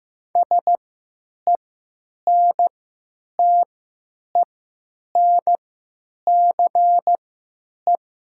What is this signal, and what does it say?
Morse code, heard as a keyed tone, about 15 wpm, spaced as for 9 wpm, 700 Hz. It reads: SENTENCE
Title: 3x Character Spacing - Top 300 Words - 15wpm